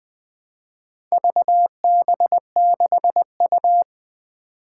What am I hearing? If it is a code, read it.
Morse code, heard as a keyed tone, about 20 wpm, 700 Hz. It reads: VB6U
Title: Four Letter-Number Combinations - 20wpm